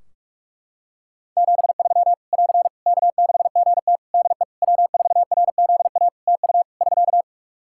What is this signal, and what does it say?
Morse code, heard as a keyed tone, about 41 wpm, 700 Hz. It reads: 73 <AR> K6GT DE W4RZA TU <SK>
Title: Practice QSO - Exchange 4 - Ending - 45wpm